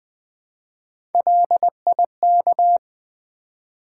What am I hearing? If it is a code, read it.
Morse code, heard as a keyed tone, about 20 wpm, 700 Hz. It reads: LIK
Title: Build A Word - 1 Letter At A Time - 4-Letter Words - 20wpm